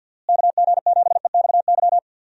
Morse code, heard as a keyed tone, about 51 wpm, 700 Hz. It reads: KG7EXY
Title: USA Call Signs - 6 Characters - Repeated 2x - 50wpm